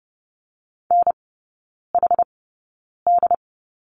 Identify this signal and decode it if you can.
Morse code, heard as a keyed tone, about 30 wpm, spaced as for 9 wpm, 700 Hz. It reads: NHD
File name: 081 Lesson 14e - Number 2 - Three Characters 30wpm